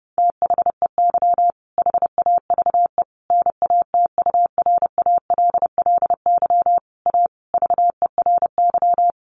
Morse code, heard as a keyed tone, about 30 wpm, 700 Hz. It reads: THEY HAVE NATURALLY A VERY
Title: Sentences from Top 1300 Words 30wpm